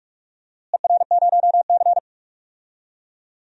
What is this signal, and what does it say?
Morse code, heard as a keyed tone, about 41 wpm, 700 Hz. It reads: ER0C
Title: DXCC-100 Call Signs - 4 Characters - Repeated 3x - 45wpm